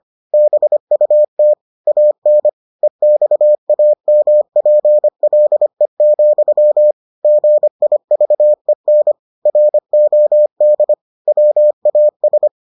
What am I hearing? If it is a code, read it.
Morse code, heard as a keyed tone, about 25 wpm, 600 Hz. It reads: BUT AN EXAMPLE, GIVEN ROD WAS